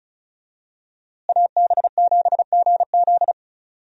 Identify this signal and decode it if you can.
Morse code, heard as a keyed tone, about 34 wpm, 700 Hz. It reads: AB7GZ
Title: USA Call Signs - 5 Characters - Repeated 2x - 35wpm